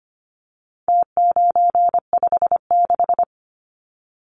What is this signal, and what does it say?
Morse code, heard as a keyed tone, about 25 wpm, 700 Hz. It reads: T956
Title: Numbers - Contest Style - 4-Digit 25wpm